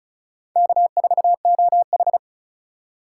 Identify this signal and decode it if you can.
Morse code, heard as a keyed tone, about 34 wpm, 700 Hz. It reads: K4OH
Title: USA Call Signs - 4 Characters - 3x Character Spacing - 35wpm